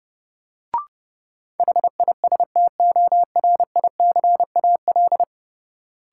Morse code, heard as a keyed tone, about 32 wpm, 700 Hz. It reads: HISTORICAL
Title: Sentences from Top 600 Words - New Word Drill - 30wpm